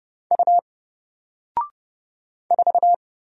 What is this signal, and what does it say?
Morse code, heard as a keyed tone, about 30 wpm, 700 Hz. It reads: U 4
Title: Single Letter-Number Alt 30wpm